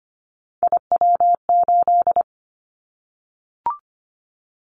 Morse code, heard as a keyed tone, about 25 wpm, 700 Hz. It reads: IW8
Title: Callsign Prefixes for Top 25 DXCC - 25wpm